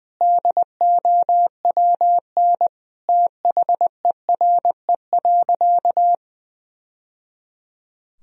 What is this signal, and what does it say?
Morse code, heard as a keyed tone, about 21 wpm, 700 Hz. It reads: DOWN THERE.
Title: Sentences from Top 1000 Words - Review Part 4 - 20wpm